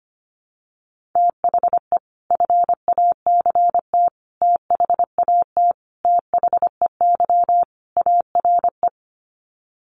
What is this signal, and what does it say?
Morse code, heard as a keyed tone, about 25 wpm, 700 Hz. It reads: THE FACT THAT THEY ARE
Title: Sets of 5 Words Encore2 25wpm